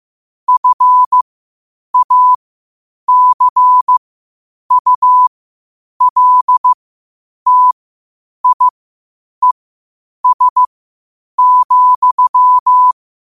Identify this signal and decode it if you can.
Morse code, heard as a keyed tone, about 15 wpm, spaced as for 8 wpm, 1000 Hz. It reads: FACULTIES,
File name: qotdfiles / 8wpm